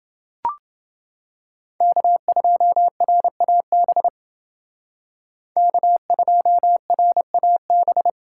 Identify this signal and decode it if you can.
Morse code, heard as a keyed tone, about 28 wpm, 700 Hz. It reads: K2RAB K2RAB